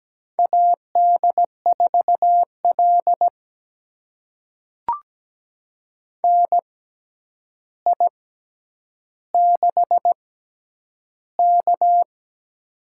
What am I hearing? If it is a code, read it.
Morse code, heard as a keyed tone, about 17 wpm, spaced as for 9 wpm, 700 Hz. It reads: AD4L NI6K
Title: USA Call Signs - 4 Characters - 6x Character Spacing - 17wpm